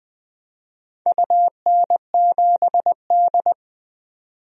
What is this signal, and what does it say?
Morse code, heard as a keyed tone, about 20 wpm, 700 Hz. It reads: UN7D
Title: DXCC-100 Call Signs - 4 Characters - 20wpm